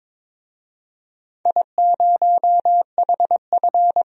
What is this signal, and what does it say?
Morse code, heard as a keyed tone, about 22 wpm, 700 Hz. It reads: I0HF
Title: DXCC-100 Call Signs - 4 Characters - 22wpm